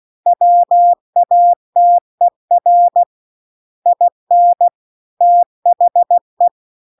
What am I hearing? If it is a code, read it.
Morse code, heard as a keyed tone, about 16 wpm, 700 Hz. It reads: WATER IN THE